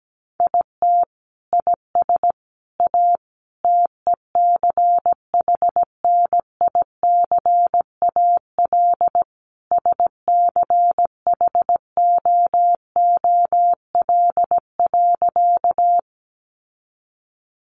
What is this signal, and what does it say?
Morse code, heard as a keyed tone, about 17 wpm, 700 Hz. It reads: IT IS A TECHNICAL SCHOOL.
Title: Sentences from Top 1100 Words 17wpm